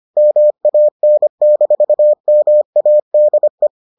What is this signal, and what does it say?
Morse code, heard as a keyed tone, about 25 wpm, 600 Hz. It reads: MAN-MADE